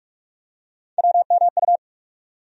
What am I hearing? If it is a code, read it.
Morse code, heard as a keyed tone, about 46 wpm, 700 Hz. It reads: WMU
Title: USA Call Sign Postfixes - 3 Letters 45wpm